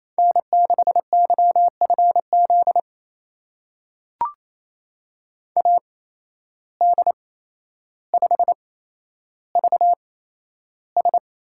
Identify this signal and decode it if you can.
Morse code, heard as a keyed tone, about 27 wpm, spaced as for 12 wpm, 700 Hz. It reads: N6YFZ AD5VS
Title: USA Call Signs - 8x Character Spacing - 28wpm